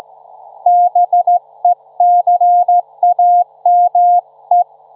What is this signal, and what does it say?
Morse code, heard as a keyed tone, about 15 wpm, 710 Hz. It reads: BECAME
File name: Trivia-ZL2WT